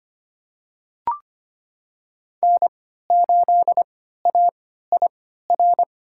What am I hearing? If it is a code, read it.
Morse code, heard as a keyed tone, about 25 wpm, spaced as for 14 wpm, 700 Hz. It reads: N8AIR
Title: USA Call Signs - 5 Characters - 3x Character Spacing - 25wpm